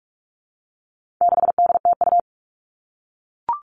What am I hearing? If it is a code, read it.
Morse code, heard as a keyed tone, about 44 wpm, 700 Hz. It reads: 6DTU
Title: All Cut Numbers - 4-Digit 45wpm